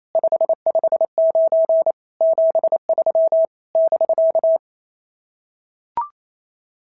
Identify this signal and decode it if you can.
Morse code, heard as a keyed tone, about 27 wpm, 650 Hz. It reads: 559 73 <BK>
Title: SOTA QSO Practice 28wpm